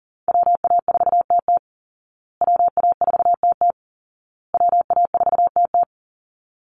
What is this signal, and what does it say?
Morse code, heard as a keyed tone, about 40 wpm, 700 Hz. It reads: WA4TT WA4TT WA4TT
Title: Call Signs Repeated 2x - 40wpm